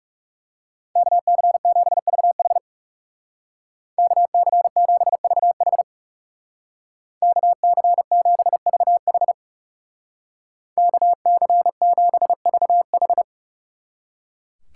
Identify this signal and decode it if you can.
Morse code, heard as a keyed tone, about 37 wpm, 700 Hz. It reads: KC7VH KC7VH KC7VH KC7VH
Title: Speed Racing - USA 2x2 Call Signs 30wpm